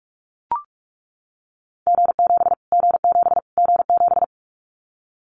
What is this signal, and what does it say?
Morse code, heard as a keyed tone, about 44 wpm, 700 Hz. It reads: G7 G7 G7
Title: Callsign Prefixes for Top 25 DXCC - Repeated 3x - 45wpm